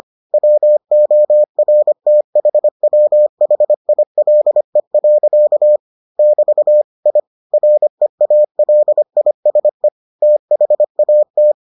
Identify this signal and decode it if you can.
Morse code, heard as a keyed tone, about 25 wpm, 600 Hz. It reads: WORTHWHILE. <BT> I REALISE THAT